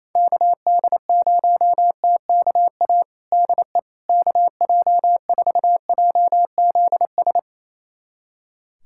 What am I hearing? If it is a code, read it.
Morse code, heard as a keyed tone, about 27 wpm, 700 Hz. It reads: KD0TKA DE KJ4JZS
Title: USA Callsign DE Callsign - 6 Characters - 28wpm